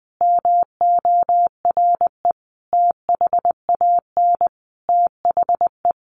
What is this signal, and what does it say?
Morse code, heard as a keyed tone, about 20 wpm, 700 Hz. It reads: MORE THAN THE
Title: Sentences from Top 1200 Words 20wpm